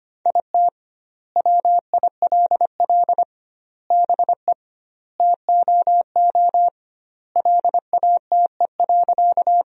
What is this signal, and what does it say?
Morse code, heard as a keyed tone, about 25 wpm, 700 Hz. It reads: IT WILL BE TOO LATE.
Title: 2x Word-Spacing - Sentences from Top 300 Words - 25wpm